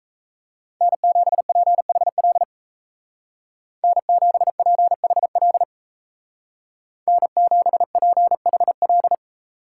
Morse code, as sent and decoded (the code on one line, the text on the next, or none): -. --... .--. .... .-.. / -. --... .--. .... .-.. / -. --... .--. .... .-..
N7PHL N7PHL N7PHL